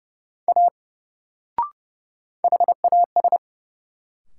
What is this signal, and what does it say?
Morse code, heard as a keyed tone, about 30 wpm, 700 Hz. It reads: A HAS
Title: Sentences from Top 100 Words - New Word Drill - 30wpm